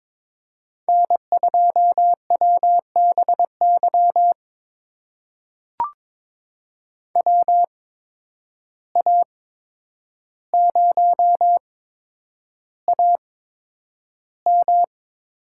Morse code, heard as a keyed tone, about 22 wpm, spaced as for 10 wpm, 700 Hz. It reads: N2WBY WA0AM
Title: USA Call Signs - 5 Characters - 8x Character Spacing - 22wpm